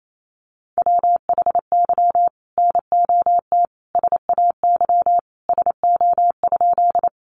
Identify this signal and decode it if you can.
Morse code, heard as a keyed tone, about 28 wpm, 700 Hz. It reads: WHY NOT SAY SO?
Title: Sentences from Top 600 Words 28wpm